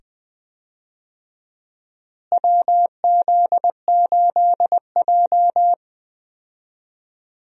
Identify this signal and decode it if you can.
Morse code, heard as a keyed tone, about 20 wpm, 700 Hz. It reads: WZ8J
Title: USA Call Signs - 4 Characters - Repeated 2x - 20wpm